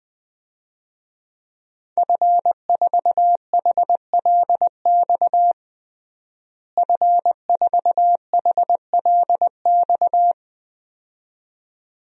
Morse code, heard as a keyed tone, about 20 wpm, 700 Hz. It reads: F4HLX F4HLX
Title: DXCC-100 Call Signs - 5 Characters - Repeated 2x - 20wpm